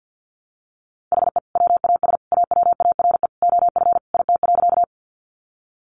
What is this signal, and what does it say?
Morse code, heard as a keyed tone, about 50 wpm, 700 Hz. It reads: HE WAS AWARE OF IT.
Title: Sentences from Top 1200 Words 50wpm